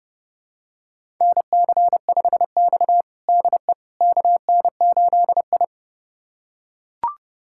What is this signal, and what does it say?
Morse code, heard as a keyed tone, about 30 wpm, 700 Hz. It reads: NC5X DE KN8I
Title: USA Callsign DE Callsign - 4 Characters - 30wpm